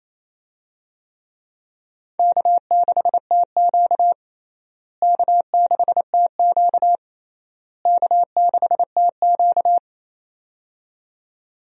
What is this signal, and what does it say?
Morse code, heard as a keyed tone, about 28 wpm, 700 Hz. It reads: K6TQ K6TQ K6TQ
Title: USA Call Signs - 4 Characters - Repeated 3x - 28wpm